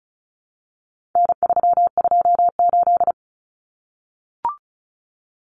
Morse code, heard as a keyed tone, about 34 wpm, 700 Hz. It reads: N328
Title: Common Cut Numbers - 4-Digit 35wpm